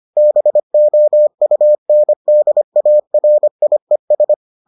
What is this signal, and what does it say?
Morse code, heard as a keyed tone, about 25 wpm, 600 Hz. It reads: BOUNDARIES